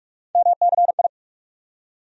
Morse code, heard as a keyed tone, about 47 wpm, 700 Hz. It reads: MCI
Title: USA Call Sign Postfixes - 3 Letters 45wpm